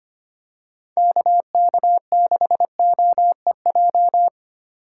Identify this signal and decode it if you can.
Morse code, heard as a keyed tone, about 25 wpm, 700 Hz. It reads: KK6OEJ
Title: USA Call Signs - 6 Characters - 3x Character Spacing - 25wpm